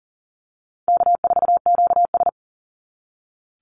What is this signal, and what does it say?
Morse code, heard as a keyed tone, about 40 wpm, 700 Hz. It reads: K4QS
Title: NA SOTA Activator Callsigns 40wpm